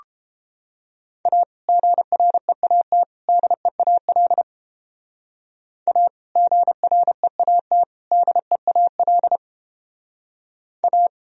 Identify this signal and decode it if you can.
Morse code, heard as a keyed tone, about 31 wpm, 700 Hz. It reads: A GREAT DEAL A GREAT DEAL A